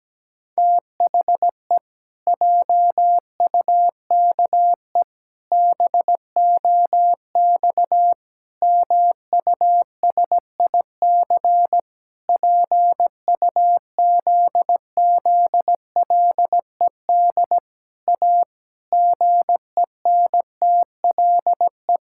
Morse code, heard as a keyed tone, about 17 wpm, 700 Hz. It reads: THE JUKE BOX MUSIC PUZZLED A GENTLE
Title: Pangram Challenge v2 - 17wpm